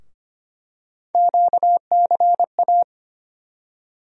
Morse code, heard as a keyed tone, about 25 wpm, 700 Hz. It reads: QCA
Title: USA Call Sign Postfixes - 3 Letters 25wpm